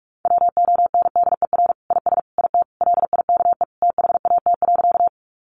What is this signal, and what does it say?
Morse code, heard as a keyed tone, about 44 wpm, 700 Hz. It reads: WONDER IS IT LIKE THAT.
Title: Sentences from Top 500 Words 45wpm